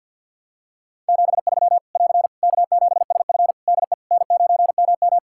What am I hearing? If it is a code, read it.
Morse code, heard as a keyed tone, about 50 wpm, 700 Hz. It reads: 73 <AR> K7SF DE N9MM
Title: K1USN SST QSO Practice 50wpm